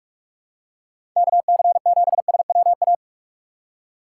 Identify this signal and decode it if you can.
Morse code, heard as a keyed tone, about 46 wpm, 700 Hz. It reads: KC7SWA
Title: USA Call Signs - 6 Characters - 2x Character Spacing - 45wpm